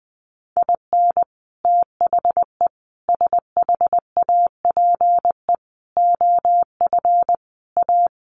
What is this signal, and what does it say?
Morse code, heard as a keyed tone, about 19 wpm, 700 Hz. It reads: IN THE SHAPE OF A
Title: Sets of 5 Words Encore 20wpm